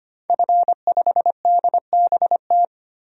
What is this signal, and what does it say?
Morse code, heard as a keyed tone, about 25 wpm, 700 Hz. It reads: F5DBT
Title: DXCC-100 Call Signs - 5 Characters - 25wpm